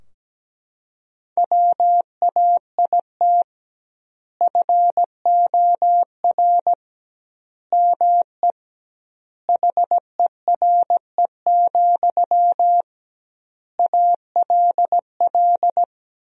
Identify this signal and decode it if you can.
Morse code, heard as a keyed tone, about 17 wpm, 700 Hz. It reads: WAIT FOR ME HERE, ALL
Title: 2x Word-Spacing - Sentences from Top 500 Words - 17wpm